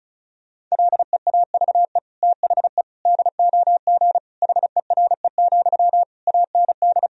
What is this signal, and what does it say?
Morse code, heard as a keyed tone, about 36 wpm, 700 Hz. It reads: LEAVE THE DOG HERE, AND
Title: Sentences from Top 1000 Words - Review Part 2 - 35wpm